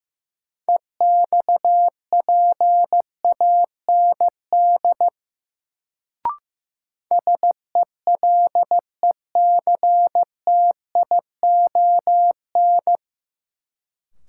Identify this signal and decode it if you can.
Morse code, heard as a keyed tone, about 15 wpm, 700 Hz. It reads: EXPAND SELECTION
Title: Sentences from Top 1400 Words - New Word Drill - 15wpm